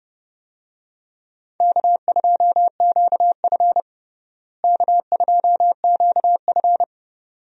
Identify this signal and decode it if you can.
Morse code, heard as a keyed tone, about 30 wpm, 700 Hz. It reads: K2QF K2QF
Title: USA Call Signs - 4 Characters - Repeated 2x - 30wpm